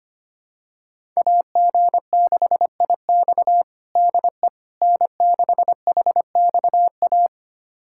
Morse code, heard as a keyed tone, about 25 wpm, 700 Hz. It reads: AG6IX DE N6HXA
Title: USA Callsign DE Callsign - 5 Characters - 25wpm